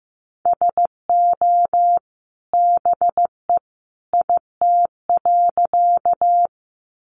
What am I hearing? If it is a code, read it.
Morse code, heard as a keyed tone, about 15 wpm, 700 Hz. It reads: SO BE IT.
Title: Sentences from Top 100 Words 15wpm